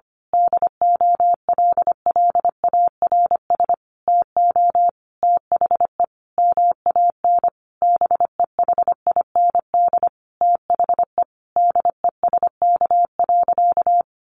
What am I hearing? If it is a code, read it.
Morse code, heard as a keyed tone, about 25 wpm, 700 Hz. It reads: DOLLARS TO THE MAN BEHIND THE DESK.
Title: A Sound Of Thunder 25wpm